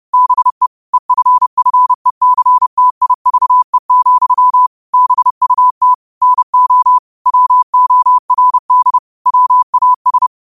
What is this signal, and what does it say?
Morse code, heard as a keyed tone, about 30 wpm, 1000 Hz. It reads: BE EFFECTIVE, BUT NO WORD WAS
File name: qotdfiles / 30wpm